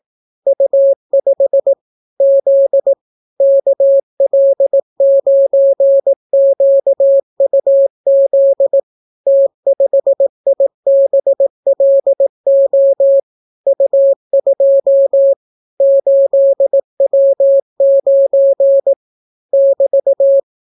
Morse code, heard as a keyed tone, about 18 wpm, 550 Hz. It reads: U5 Z KL9QUZ T5IBLO U2 8W9 <BT>